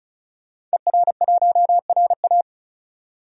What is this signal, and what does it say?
Morse code, heard as a keyed tone, about 34 wpm, 700 Hz. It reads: ER1RA
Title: DXCC-100 Call Signs - 5 Characters - 35wpm